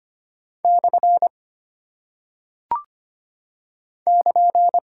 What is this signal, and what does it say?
Morse code, heard as a keyed tone, about 25 wpm, 700 Hz. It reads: / <KN>
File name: Single Punctuation-Prosign - 25wpm